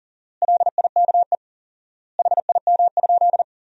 Morse code, heard as a keyed tone, about 40 wpm, 700 Hz. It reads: LIKE HIM?
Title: 4x Word-Spacing - Sentences from Top 100 Words - 40wpm